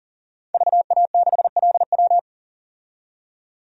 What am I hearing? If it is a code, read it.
Morse code, heard as a keyed tone, about 40 wpm, 700 Hz. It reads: VA6LW
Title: DXCC-100 Call Signs - 5 Characters - 40wpm